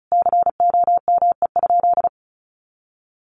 Morse code, heard as a keyed tone, about 34 wpm, 700 Hz. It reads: COME?
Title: Sentences from Top 1400 Words 35wpm